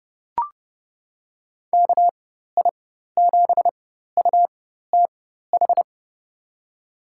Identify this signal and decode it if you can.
Morse code, heard as a keyed tone, about 30 wpm, spaced as for 14 wpm, 700 Hz. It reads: KI7UTH
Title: USA Call Signs - 6 Characters - 4x Character Spacing - 30wpm